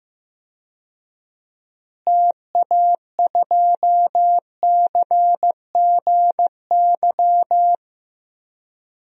Morse code, heard as a keyed tone, about 15 wpm, 700 Hz. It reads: TA2CGY
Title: DXCC-100 Call Signs - 6 Characters - Repeated 2x - 15wpm